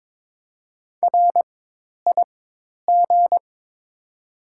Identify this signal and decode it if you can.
Morse code, heard as a keyed tone, about 22 wpm, spaced as for 10 wpm, 700 Hz. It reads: RIG
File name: Top 90 QSO Elements - 4x Character Spacing - 22wpm